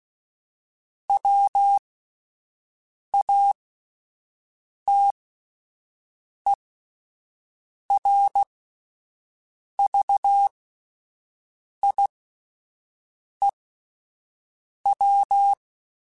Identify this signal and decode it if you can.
Morse code, heard as a keyed tone, about 16 wpm, spaced as for 5 wpm, 800 Hz. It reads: WATERVIEW